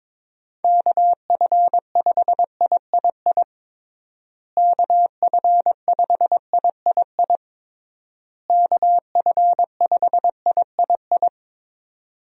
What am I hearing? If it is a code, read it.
Morse code, heard as a keyed tone, about 22 wpm, 700 Hz. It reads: KF5III KF5III KF5III
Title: USA Call Signs - 6 Characters - Repeated 3x - 22wpm